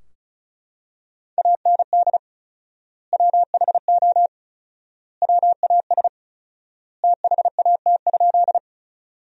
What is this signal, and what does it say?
Morse code, heard as a keyed tone, about 34 wpm, 700 Hz. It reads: AND WHO WAS THAT?
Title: 4x Word-Spacing - Sentences from Top 100 Words - 35wpm